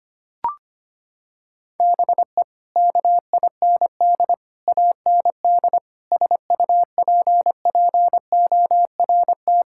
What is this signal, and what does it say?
Morse code, heard as a keyed tone, about 25 wpm, 700 Hz. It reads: BE KIND AND SUPPORT
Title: Sentences from Top 1000 Words - Review Part 5 - 25wpm